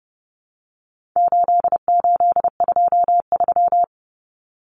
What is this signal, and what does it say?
Morse code, heard as a keyed tone, about 30 wpm, 700 Hz. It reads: 8823
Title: Numbers - 4-Digit 30wpm